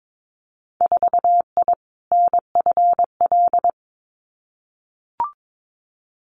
Morse code, heard as a keyed tone, about 22 wpm, 700 Hz. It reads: 4I NFL
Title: Winter Field Day Exchanges Encore 22wpm